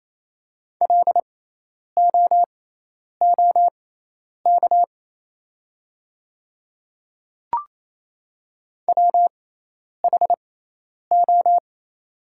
Morse code, heard as a keyed tone, about 28 wpm, spaced as for 10 wpm, 700 Hz. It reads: LOOK WHO
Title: 6x Character Spacing - Top 100 Words - 28wpm